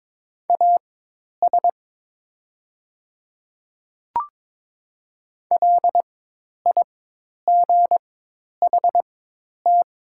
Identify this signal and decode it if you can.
Morse code, heard as a keyed tone, about 22 wpm, spaced as for 10 wpm, 700 Hz. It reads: AS LIGHT
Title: 4x Character Spacing - Top 500 Words - 22wpm